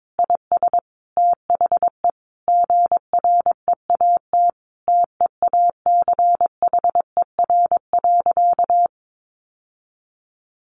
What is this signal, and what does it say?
Morse code, heard as a keyed tone, about 22 wpm, 700 Hz. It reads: IS THE GREAT TEACHER.
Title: Sentences from Top 900 Words 22wpm